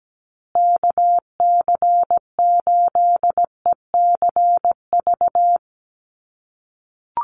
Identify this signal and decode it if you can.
Morse code, heard as a keyed tone, about 17 wpm, 700 Hz. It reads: KC8ECV
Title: USA Call Signs Repeated 2x 17wpm